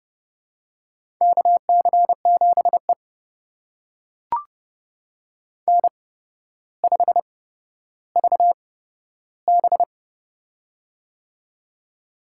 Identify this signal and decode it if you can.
Morse code, heard as a keyed tone, about 30 wpm, spaced as for 13 wpm, 700 Hz. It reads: KC7E N5VB
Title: USA Call Signs - 4 Characters - 8x Character Spacing - 30wpm